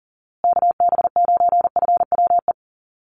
Morse code, heard as a keyed tone, about 40 wpm, 700 Hz. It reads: KB9FWE
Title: USA Call Signs Repeated 2x 40wpm